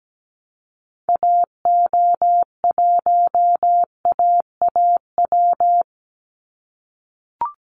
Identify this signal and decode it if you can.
Morse code, heard as a keyed tone, about 17 wpm, 700 Hz. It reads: AO1AAW
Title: Call Signs - 17wpm